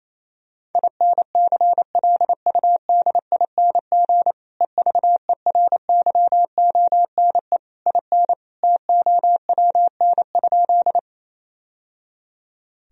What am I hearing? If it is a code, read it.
Morse code, heard as a keyed tone, about 28 wpm, 700 Hz. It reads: INCLUDING EVERYONE IN TOWN?
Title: Sentences from Top 1000 Words - Review Part 4 - 28wpm